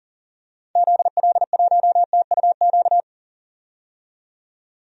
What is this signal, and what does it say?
Morse code, heard as a keyed tone, about 38 wpm, 700 Hz. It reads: ZL1TUQ
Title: DXCC-100 Call Signs - 6 Characters - 40wpm